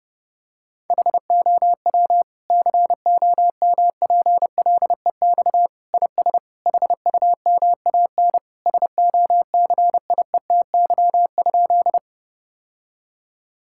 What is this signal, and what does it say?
Morse code, heard as a keyed tone, about 32 wpm, 700 Hz. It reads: HOW COMPLEX IS HUMAN SOCIETY?
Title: Sentences from Top 1000 Words - Review Part 6 - 30wpm